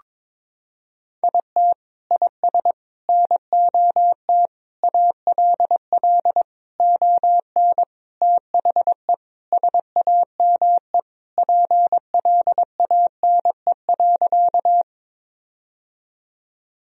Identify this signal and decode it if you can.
Morse code, heard as a keyed tone, about 22 wpm, 700 Hz. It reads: IT IS NOT ALL ON THE SAME PLANE.
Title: Sentences from Top 1000 Words - Review Part 3 - 22wpm